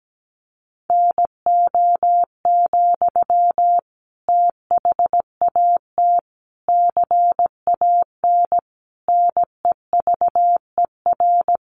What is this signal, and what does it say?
Morse code, heard as a keyed tone, about 17 wpm, 700 Hz. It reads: NO, THAT CAN NEVER
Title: Sentences from Top 300 Words 17wpm